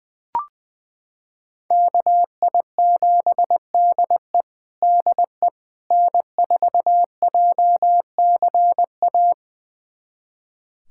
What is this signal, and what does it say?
Morse code, heard as a keyed tone, about 21 wpm, 700 Hz. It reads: KI7DE DE N4JCA
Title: USA Callsign DE Callsign - 5 Characters - 20wpm